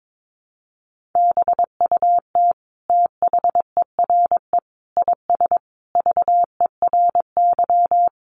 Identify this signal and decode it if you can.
Morse code, heard as a keyed tone, about 22 wpm, 700 Hz. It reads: BUT THERE IS VERY
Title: Sentences from Top 1000 Words 22wpm